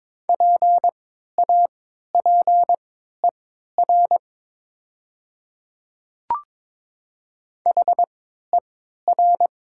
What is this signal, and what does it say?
Morse code, heard as a keyed tone, about 22 wpm, spaced as for 12 wpm, 700 Hz. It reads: PAPER HER
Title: 3x Character Spacing - Top 500 Words - 22wpm